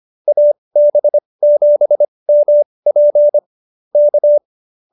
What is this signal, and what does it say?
Morse code, heard as a keyed tone, about 25 wpm, spaced as for 20 wpm, 590 Hz. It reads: AB7MP K